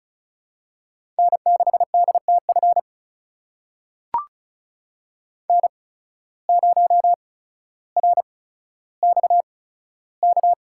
Morse code, heard as a keyed tone, about 34 wpm, spaced as for 15 wpm, 700 Hz. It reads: N6DTF N0RXK
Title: USA Call Signs - 8x Character Spacing - 35wpm